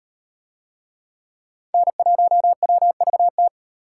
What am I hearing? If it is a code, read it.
Morse code, heard as a keyed tone, about 34 wpm, 700 Hz. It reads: N1WVT